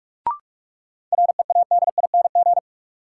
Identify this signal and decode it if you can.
Morse code, heard as a keyed tone, about 42 wpm, 700 Hz. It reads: READING
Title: Sentences from Top 600 Words - New Word Drill - 45wpm